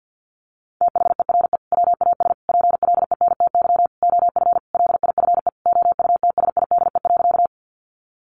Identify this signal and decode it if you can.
Morse code, heard as a keyed tone, about 50 wpm, 700 Hz. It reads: THERE WAS PLENTY OF LIFE OUTSIDE.
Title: Sentences from Top 1300 Words 50wpm